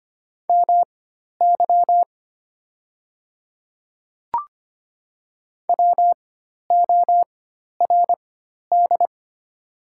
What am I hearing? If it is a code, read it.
Morse code, heard as a keyed tone, about 25 wpm, spaced as for 12 wpm, 700 Hz. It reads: MY WORD
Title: 4x Character Spacing - Top 100 Words - 25wpm